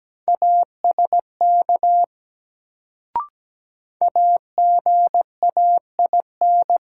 Morse code, heard as a keyed tone, about 17 wpm, 700 Hz. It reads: ASK AGAIN
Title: Sentences from Top 200 Words - New Word Drill - 17wpm